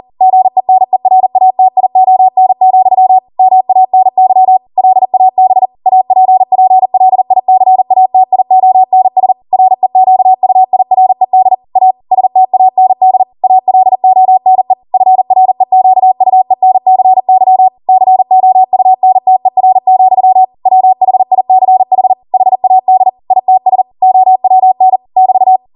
Morse code, heard as a keyed tone, about 40 wpm, 750 Hz. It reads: GENERATION, MANY LAB APPLICATIONS REQUIRED A STAND ALONE FREQUENCY COUNTER, WHICH HAD ITS OWN <BT>